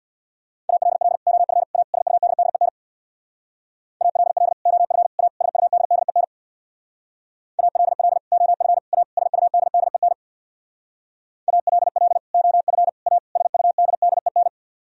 Morse code, heard as a keyed tone, about 63 wpm, 700 Hz. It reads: ALL OF A SUDDEN ALL OF A SUDDEN ALL OF A SUDDEN ALL OF A SUDDEN